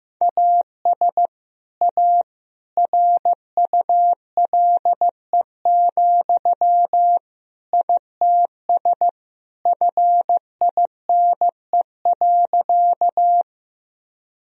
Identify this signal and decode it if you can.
Morse code, heard as a keyed tone, about 15 wpm, 700 Hz. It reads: AS A RULE, ITS FINE.
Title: Sentences from Top 1000 Words - Review Part 2 - 15wpm